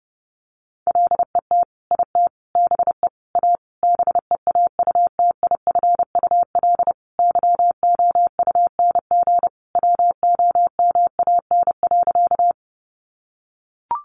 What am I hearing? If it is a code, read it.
Morse code, heard as a keyed tone, about 28 wpm, 700 Hz. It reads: LET IT BE A BEAUTIFUL YOUNG WOMAN.
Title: Sentences from Top 1100 Words 30wpm